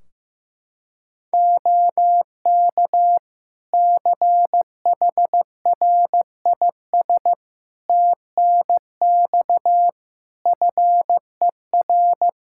Morse code, heard as a keyed tone, about 15 wpm, 700 Hz. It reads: OK CHRIS TNX FER
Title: Practice QSO - Exchange 4 - Ending - 15wpm